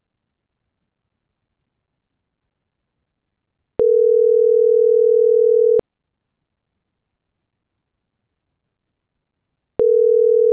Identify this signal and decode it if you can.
Ringback tone (North America), 440 + 480 Hz.